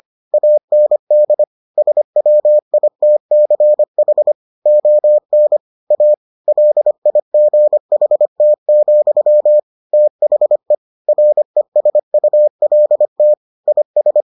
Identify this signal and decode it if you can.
Morse code, heard as a keyed tone, about 25 wpm, 600 Hz. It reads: AND SWITCH ON A LIGHT, THE RESULT IS